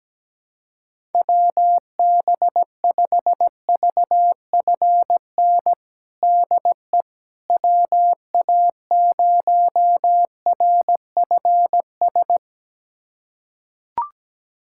Morse code, heard as a keyed tone, about 17 wpm, 700 Hz. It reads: WB5VFN DE WA0RFS
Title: USA Callsign DE Callsign - 6 Characters - 17wpm